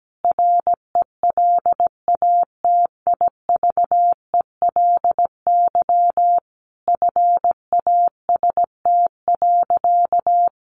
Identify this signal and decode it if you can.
Morse code, heard as a keyed tone, about 17 wpm, 700 Hz. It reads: RELATIVELY FAST.
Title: Sentences from Top 1100 Words 17wpm